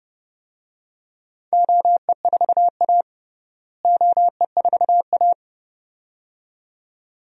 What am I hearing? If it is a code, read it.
Morse code, heard as a keyed tone, about 30 wpm, 700 Hz. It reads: OE4A OE4A
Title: DXCC-100 Call Signs - 4 Characters - Repeated 2x - 30wpm